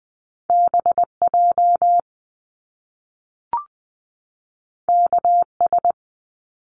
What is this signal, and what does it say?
Morse code, heard as a keyed tone, about 20 wpm, 700 Hz. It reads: BJ KS
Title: Top Two Letter Combinations in Words 20wpm